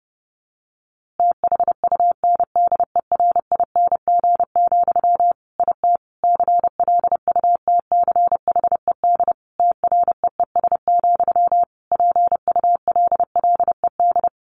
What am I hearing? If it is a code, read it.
Morse code, heard as a keyed tone, about 30 wpm, 700 Hz. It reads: THUNDERING, IT CLUTCHED TREES, PULLED
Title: A Sound Of Thunder 30wpm